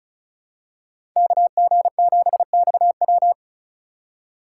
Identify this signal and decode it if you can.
Morse code, heard as a keyed tone, about 34 wpm, 700 Hz. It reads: KG7XW
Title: USA Call Signs - 5 Characters - 6x Character Spacing - 35wpm